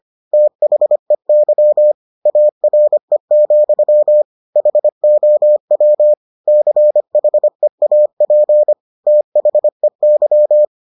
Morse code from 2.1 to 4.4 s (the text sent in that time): ARE,